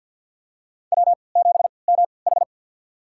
Morse code, heard as a keyed tone, about 53 wpm, spaced as for 29 wpm, 700 Hz. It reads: W7MH
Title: USA Call Signs - 4 Characters - 3x Character Spacing - 50wpm